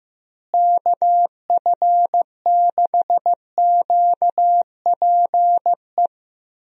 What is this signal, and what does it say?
Morse code, heard as a keyed tone, about 15 wpm, 700 Hz. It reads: KF6QPE
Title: USA Call Signs - 6 Characters - Repeated 3x - 15wpm